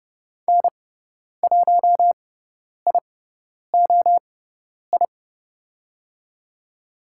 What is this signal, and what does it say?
Morse code, heard as a keyed tone, about 30 wpm, spaced as for 10 wpm, 700 Hz. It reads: N1IOI
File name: Lesson 16g - Letter c - Call Signs 30wpm